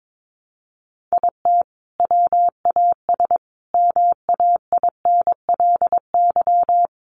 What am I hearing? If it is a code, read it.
Morse code, heard as a keyed tone, about 22 wpm, 700 Hz. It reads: IT WAS MAINLY
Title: Sentences from Top 1400 Words 22wpm